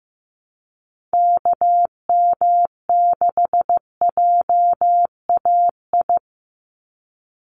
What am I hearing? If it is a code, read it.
Morse code, heard as a keyed tone, about 15 wpm, 700 Hz. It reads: KM6JAI
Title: USA Call Signs 15wpm